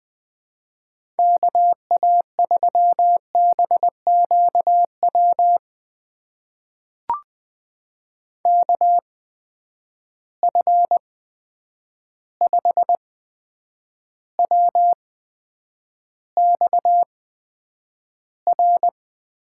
Morse code, heard as a keyed tone, about 20 wpm, spaced as for 9 wpm, 700 Hz. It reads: KA3BQW KF5WXR
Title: USA Call Signs - 6 Characters - 8x Character Spacing - 20wpm